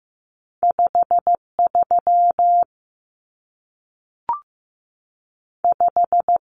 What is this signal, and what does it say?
Morse code, heard as a keyed tone, about 15 wpm, spaced as for 3 wpm, 700 Hz. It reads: 5 35
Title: Numbers - 2-Digit 15wpm